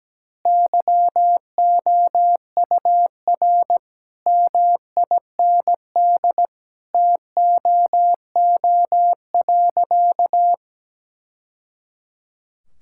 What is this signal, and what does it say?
Morse code, heard as a keyed tone, about 17 wpm, 700 Hz. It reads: YOUR MIND TOO.
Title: Sentences from Top 1000 Words - Review Part 6 - 17wpm